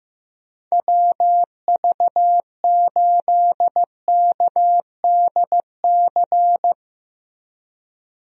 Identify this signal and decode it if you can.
Morse code, heard as a keyed tone, about 15 wpm, 700 Hz. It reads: WV8KDC
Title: USA Call Signs - 6 Characters - Repeated 2x - 15wpm